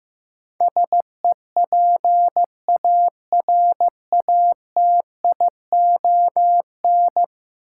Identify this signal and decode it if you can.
Morse code, heard as a keyed tone, about 15 wpm, 700 Hz. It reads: SEPARATION
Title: Sets of 5 Words - New Word Drill - 15wpm